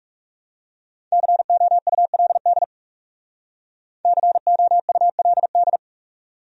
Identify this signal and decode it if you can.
Morse code, heard as a keyed tone, about 39 wpm, 700 Hz. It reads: COULD COULD